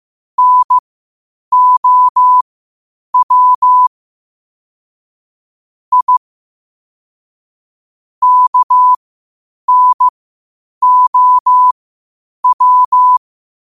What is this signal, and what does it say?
Morse code, heard as a keyed tone, about 15 wpm, spaced as for 8 wpm, 1000 Hz. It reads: NOW I KNOW
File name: qotdfiles / 8wpm